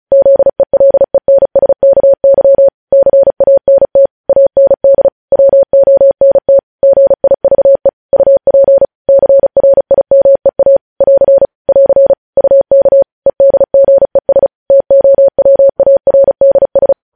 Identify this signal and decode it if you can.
Morse code, heard as a keyed tone, about 36 wpm, 550 Hz. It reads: ZELENSKY CANT AND WONT GIVE UP CRIMEA <AR> <AR> UK EDGES TOWARDS
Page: News Headlines in Morse Code at 35 WPM